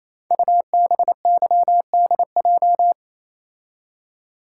Morse code, heard as a keyed tone, about 27 wpm, 700 Hz. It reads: UBYDJ
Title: Code Groups - 5 Letters - 28wpm